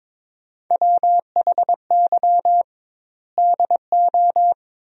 Morse code, heard as a keyed tone, about 22 wpm, 700 Hz. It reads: WHY DO
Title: 2x Word-Spacing - Sentences from Top 200 Words - 22wpm